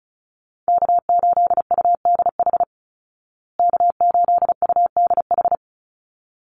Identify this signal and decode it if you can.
Morse code, heard as a keyed tone, about 34 wpm, 700 Hz. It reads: K8UDH K8UDH
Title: Call Signs Repeated 3x - 35wpm